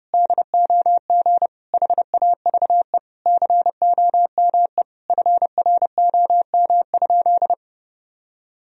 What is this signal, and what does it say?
Morse code, heard as a keyed tone, about 32 wpm, 700 Hz. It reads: DOG HAVE COME FROM?
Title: Sentences from Top 1000 Words - Review Part 2 - 30wpm